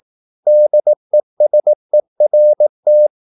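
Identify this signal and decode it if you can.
Morse code, heard as a keyed tone, about 18 wpm, 600 Hz. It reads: DESERT
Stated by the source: Morse practice at 18 wpm, English text, from CW med Gnister.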